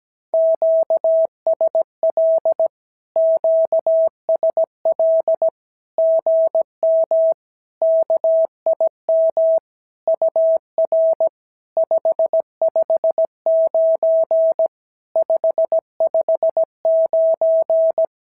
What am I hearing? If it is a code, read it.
Morse code, heard as a keyed tone, about 17 wpm, 650 Hz. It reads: QSL QSL GM KIM UR 559 559
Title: SOTA QSO Practice 17wpm